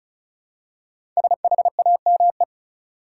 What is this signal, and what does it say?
Morse code, heard as a keyed tone, about 36 wpm, 700 Hz. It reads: SHAME
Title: Five Letter Words - 35wpm